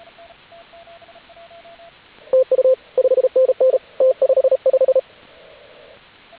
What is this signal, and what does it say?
Morse code, heard as a keyed tone, about 33 wpm, 520 Hz, with radio noise. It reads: TU 5NN T55
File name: AUDIO2019 / HG8DX CW